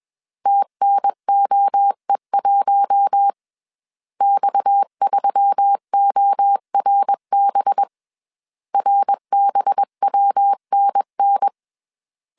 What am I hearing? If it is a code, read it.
Morse code, heard as a keyed tone, about 21 wpm, 800 Hz. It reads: TNOE1 X3ORB RBWNN
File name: CW / DET / 1 6 1500C S C 20wpm 2116